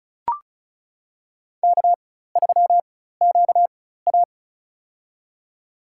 Morse code, heard as a keyed tone, about 36 wpm, spaced as for 16 wpm, 700 Hz. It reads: K3QA
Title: USA Call Signs - 4 Characters - 4x Character Spacing - 35wpm